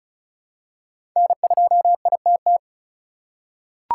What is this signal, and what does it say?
Morse code, heard as a keyed tone, about 36 wpm, 700 Hz. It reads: N2ITT
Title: USA Call Signs - 5 Characters - 4x Character Spacing - 35wpm